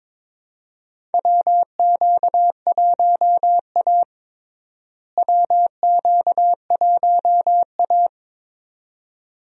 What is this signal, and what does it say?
Morse code, heard as a keyed tone, about 22 wpm, 700 Hz. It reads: WQ1A WQ1A
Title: USA Call Signs - 4 Characters - Repeated 2x - 22wpm